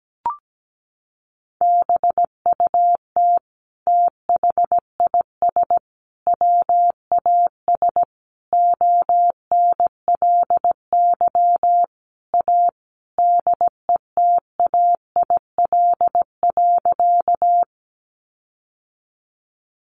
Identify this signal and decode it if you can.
Morse code, heard as a keyed tone, about 17 wpm, 700 Hz. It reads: BUT THIS WAS ONLY A DETAIL.
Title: Sentences from Top 1500 Words 17wpm